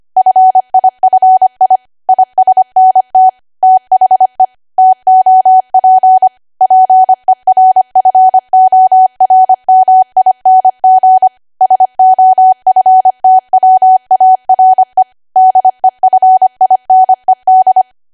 Morse code, heard as a keyed tone, about 25 wpm, 750 Hz. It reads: FIFI ISNT THE TOP PERFORMING SOFTWARE DEFINED
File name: cwa / W1AW-Level3 / 25WPM